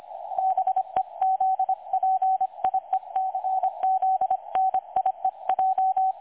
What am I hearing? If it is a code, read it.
Morse code, heard as a keyed tone, about 25 wpm, 760 Hz. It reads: BEZPIECZNIEJ
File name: dlazaawansowanych / nagraniazpasma / websdr recording 3565.0kHz